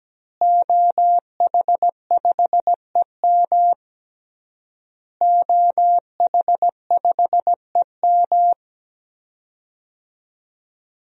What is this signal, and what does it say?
Morse code, heard as a keyed tone, about 17 wpm, 700 Hz. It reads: OH5EM OH5EM
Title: DXCC-100 Call Signs - 5 Characters - Repeated 3x - 17wpm